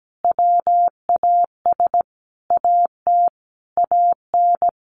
Morse code, heard as a keyed tone, about 17 wpm, 700 Hz. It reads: WAS AT AN
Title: Sentences from Top 1500 Words 17wpm